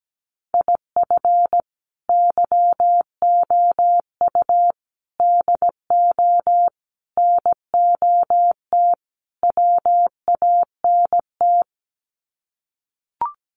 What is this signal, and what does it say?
Morse code, heard as a keyed tone, about 17 wpm, 700 Hz. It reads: IF YOU DO NOT WANT
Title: Sets of 5 Words 17wpm